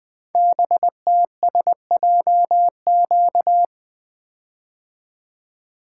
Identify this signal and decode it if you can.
Morse code, heard as a keyed tone, about 20 wpm, 700 Hz. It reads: BTSJQ
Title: Code Groups - 5 Letters - 20wpm